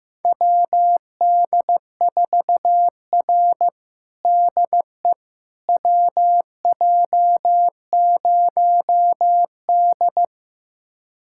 Morse code, heard as a keyed tone, about 15 wpm, 700 Hz. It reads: WD4R DE WJ0D
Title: USA Callsign DE Callsign - 4 Characters - 15wpm